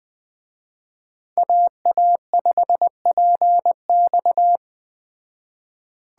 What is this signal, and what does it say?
Morse code, heard as a keyed tone, about 20 wpm, 700 Hz. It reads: AA5PX
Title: Speed Racing - USA 2x2 Call Signs 17wpm